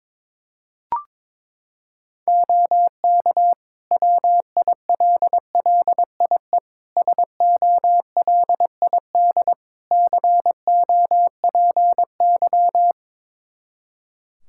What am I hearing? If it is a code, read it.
Morse code, heard as a keyed tone, about 22 wpm, 700 Hz. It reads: OK WILLIE SOLID COPY
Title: Practice QSO - Exchange 3 - More Info - 22wpm